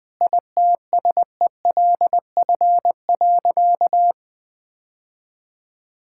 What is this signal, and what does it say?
Morse code, heard as a keyed tone, about 21 wpm, 700 Hz. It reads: ITSELF.
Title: Sentences from Top 1000 Words - Review Part 6 - 20wpm